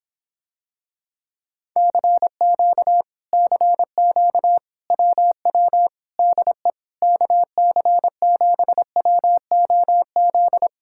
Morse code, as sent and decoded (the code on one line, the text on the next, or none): -.-. --.- / -.-. --.- / .-- .-- / -.. / -.- -.-. --... .-- --- --..
CQ CQ WW DE KC7WOZ